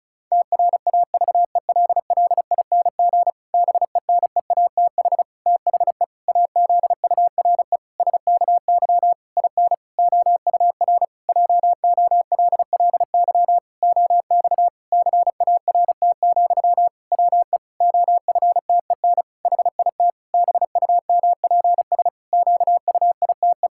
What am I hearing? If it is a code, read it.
Morse code, heard as a keyed tone, about 36 wpm, 700 Hz. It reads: TRAVELLING BENEATH THE AZURE SKY IN OUR JOLLY OX CART, WE OFTEN HIT BUMPS QUITE
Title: Pangram Challenge v2 - 35wpm